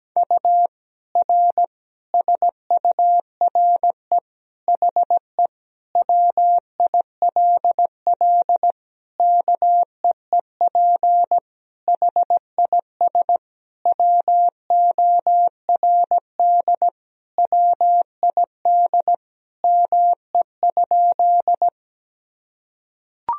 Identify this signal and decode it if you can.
Morse code, heard as a keyed tone, about 17 wpm, 700 Hz. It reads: U R SURE HE WILL KEEP HIS WORD WID ME?
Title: Sentences from Top 300 Words and Abbreviations 17wpm